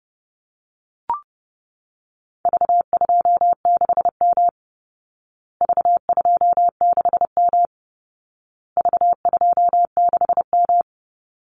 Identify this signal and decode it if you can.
Morse code, heard as a keyed tone, about 28 wpm, 700 Hz. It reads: V26M V26M V26M
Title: Call Signs Repeated 2x - 30wpm